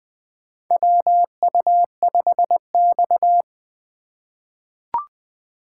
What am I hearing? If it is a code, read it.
Morse code, heard as a keyed tone, about 20 wpm, 700 Hz. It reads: WU5X
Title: USA Call Signs - 4x Character Spacing - 20wpm